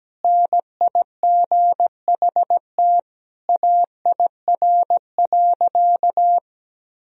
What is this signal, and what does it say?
Morse code, heard as a keyed tone, about 17 wpm, 700 Hz. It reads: NIGHT AIR.
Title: Sentences from Top 1000 Words - Review Part 1 - 17wpm